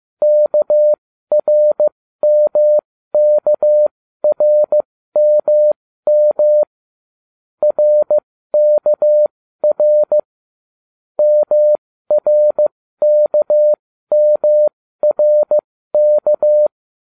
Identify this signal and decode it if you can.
Morse code, heard as a keyed tone, about 15 wpm, spaced as for 13 wpm, 600 Hz. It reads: KRMKRMM RKR MRKMRK